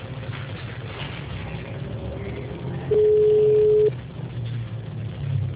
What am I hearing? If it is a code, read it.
Ringback tone (Europe, 425 Hz).
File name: out- -2008- - - .29725